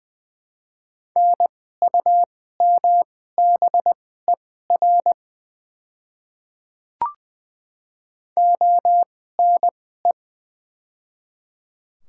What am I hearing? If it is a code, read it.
Morse code, heard as a keyed tone, about 20 wpm, spaced as for 14 wpm, 700 Hz. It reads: NUMBER ONE
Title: 2x Character Spacing - Top 100 Words - 20wpm